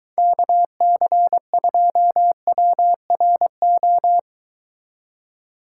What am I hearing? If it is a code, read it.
Morse code, heard as a keyed tone, about 23 wpm, 700 Hz. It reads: KC2WRO